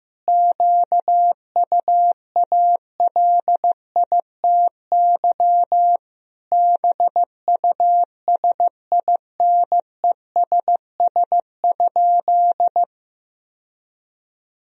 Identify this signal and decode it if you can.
Morse code, heard as a keyed tone, about 15 wpm, 700 Hz. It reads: QUALITY BUSINESS?
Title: Sentences from Top 1000 Words - Review Part 4 - 15wpm